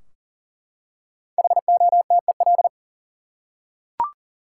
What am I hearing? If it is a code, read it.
Morse code, heard as a keyed tone, about 40 wpm, 700 Hz. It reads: HOTEL
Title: Five Letter Words - 40wpm